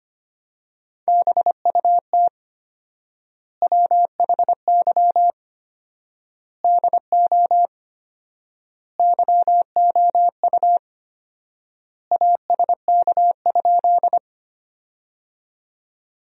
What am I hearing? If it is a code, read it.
Morse code, heard as a keyed tone, about 25 wpm, 700 Hz. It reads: BUT WHY DO YOU ASK?
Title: 4x Word-Spacing - Sentences from Top 200 Words - 25wpm